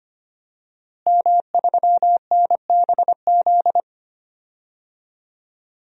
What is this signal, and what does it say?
Morse code, heard as a keyed tone, about 25 wpm, 700 Hz. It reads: M3NBZ
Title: DXCC-100 Call Signs - 5 Characters - Repeated 2x - 25wpm